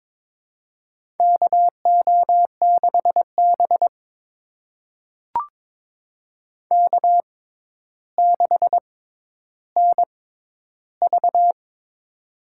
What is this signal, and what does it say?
Morse code, heard as a keyed tone, about 23 wpm, spaced as for 11 wpm, 700 Hz. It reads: KO6B K6NV
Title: USA Call Signs - 4 Characters - 6x Character Spacing - 22wpm